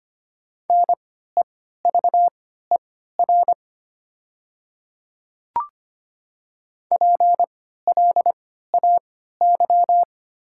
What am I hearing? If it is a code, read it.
Morse code, heard as a keyed tone, about 25 wpm, spaced as for 14 wpm, 700 Hz. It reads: NEVER PLAY
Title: 3x Character Spacing - Top 300 Words - 25wpm